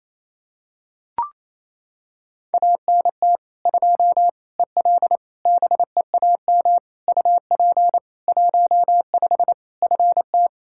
Morse code, heard as a keyed tone, about 28 wpm, 700 Hz. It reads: ANT 2 EL BEAM UP 15 FT
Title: Practice QSO - Exchange 2 - Info - 28wpm